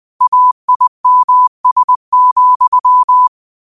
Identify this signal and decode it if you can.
Morse code, heard as a keyed tone, about 20 wpm, 1000 Hz. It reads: AIMS,
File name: qotdfiles / 20wpm